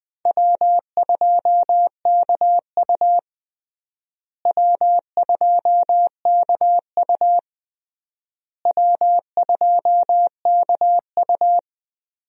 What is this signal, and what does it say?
Morse code, heard as a keyed tone, about 20 wpm, 700 Hz. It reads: W2KU W2KU W2KU
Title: USA Call Signs - 4 Characters - Repeated 3x - 20wpm